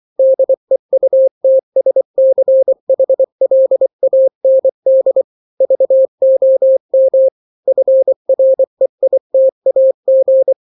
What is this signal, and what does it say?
Morse code, heard as a keyed tone, about 24 wpm, 530 Hz. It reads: DEUTSCHLAND VOM FREITAG
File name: daily / 23wpm